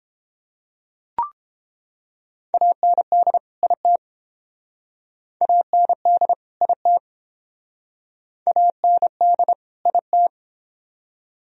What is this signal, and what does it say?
Morse code, heard as a keyed tone, about 30 wpm, 700 Hz. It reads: AND IT AND IT AND IT